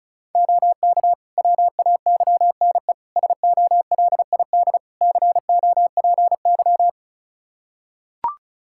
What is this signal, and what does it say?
Morse code, heard as a keyed tone, about 36 wpm, 700 Hz. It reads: OK WAYNE SOLID COPY
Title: Practice QSO - Exchange 3 - More Info - 35wpm